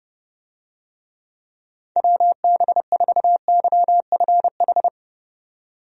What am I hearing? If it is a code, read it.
Morse code, heard as a keyed tone, about 30 wpm, 700 Hz. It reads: WB4YFH
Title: USA Call Signs - 6 Characters - 30wpm